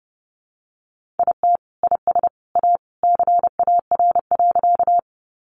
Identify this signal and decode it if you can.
Morse code, heard as a keyed tone, about 30 wpm, 700 Hz. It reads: IT IS A CAR.
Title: Easy Phrases 30wpm